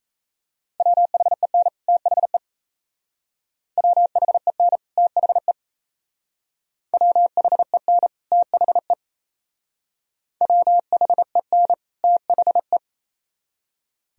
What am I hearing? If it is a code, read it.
Morse code, heard as a keyed tone, about 36 wpm, 700 Hz. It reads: WHEN THE WHEN THE WHEN THE WHEN THE